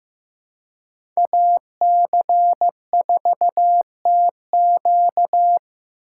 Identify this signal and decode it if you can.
Morse code, heard as a keyed tone, about 15 wpm, 700 Hz. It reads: AC4TQ
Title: USA Call Signs - 5 Characters - 4x Character Spacing - 15wpm